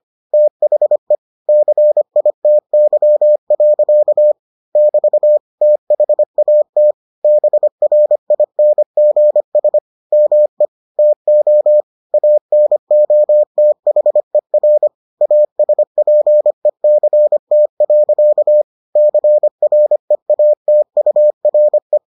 Morse code, heard as a keyed tone, about 25 wpm, 600 Hz. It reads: THE CITY. <BT> THAT BRINGS ME TO ANOTHER ASPECT. CREATURE